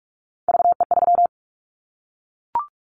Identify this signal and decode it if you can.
Morse code, heard as a keyed tone, about 46 wpm, 700 Hz. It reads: VE3
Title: Callsign Prefixes for Top 25 DXCC - 45wpm